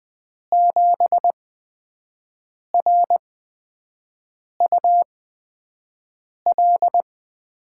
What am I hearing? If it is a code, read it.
Morse code, heard as a keyed tone, about 20 wpm, spaced as for 5 wpm, 700 Hz. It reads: 7RUL
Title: USA Call Signs - 6 Characters - 8x Character Spacing - 20wpm